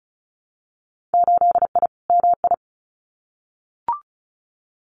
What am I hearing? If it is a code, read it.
Morse code, heard as a keyed tone, about 34 wpm, 700 Hz. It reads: 8I MI
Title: Winter Field Day Exchanges Encore 35wpm